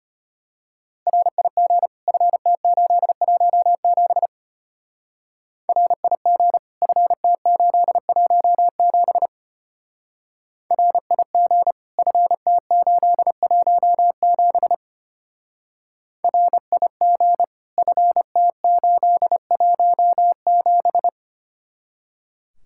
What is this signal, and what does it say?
Morse code, heard as a keyed tone, about 31 wpm, 700 Hz. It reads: RIG FT817 RIG FT817 RIG FT817 RIG FT817
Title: Speed-Racing Common QSO Phrases - 25wpm